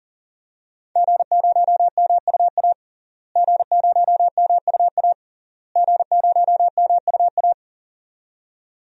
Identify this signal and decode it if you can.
Morse code, heard as a keyed tone, about 40 wpm, 700 Hz. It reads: G0MUA G0MUA G0MUA
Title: DXCC-100 Call Signs - 5 Characters - Repeated 3x - 40wpm